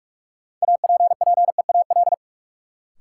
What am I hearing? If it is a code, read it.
Morse code, heard as a keyed tone, about 42 wpm, 700 Hz. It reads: APPEAL
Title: Sentences from Top 1300 Words - New Word Drill - 45wpm